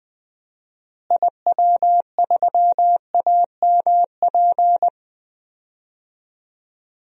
Morse code, heard as a keyed tone, about 20 wpm, 700 Hz. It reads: IW3AMP
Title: DXCC-100 Call Signs - 6 Characters - Repeated 3x - 20wpm